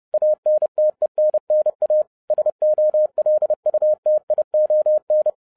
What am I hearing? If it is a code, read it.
Morse code, heard as a keyed tone, about 30 wpm, 610 Hz. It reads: ANTENNA SOLUTION